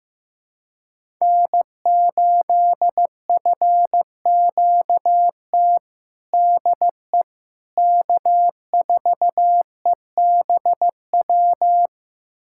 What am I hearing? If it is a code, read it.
Morse code, heard as a keyed tone, about 15 wpm, 700 Hz. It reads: N8FQT DE K4EBW
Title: USA Callsign DE Callsign - 5 Characters - 15wpm